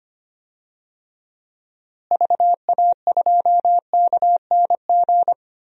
Morse code, heard as a keyed tone, about 25 wpm, 700 Hz. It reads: VA2KNG
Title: DXCC-100 Call Signs - 6 Characters - Repeated 2x - 25wpm